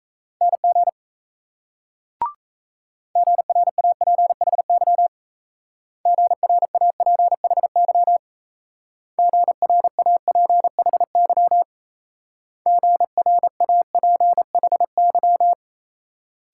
Morse code, heard as a keyed tone, about 35 wpm, 700 Hz. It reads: NG GRAPHY GRAPHY GRAPHY GRAPHY